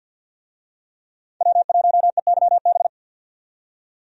Morse code, heard as a keyed tone, about 45 wpm, 700 Hz. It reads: W1EYB
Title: USA Call Signs - 5 Characters - 50wpm